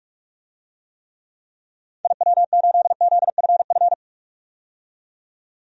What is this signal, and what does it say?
Morse code, heard as a keyed tone, about 46 wpm, 700 Hz. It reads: IW8ZFF
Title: DXCC-100 Call Signs - 6 Characters - 45wpm